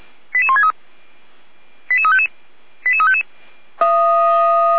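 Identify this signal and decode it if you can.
Five-tone selective call (ZVEI), heard as quick runs of five tones, tones of about 70 ms. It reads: 88163 88355 88355
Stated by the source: ZVEI five-tone call 88352 received via SDR.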